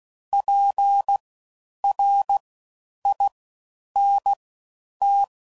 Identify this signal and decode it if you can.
Morse code, heard as a keyed tone, about 16 wpm, spaced as for 9 wpm, 800 Hz. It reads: PRINT
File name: Suffixes / Special INGs / Fast Ings